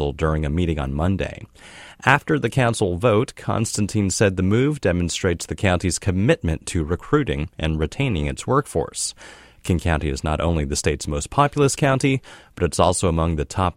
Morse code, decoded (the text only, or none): none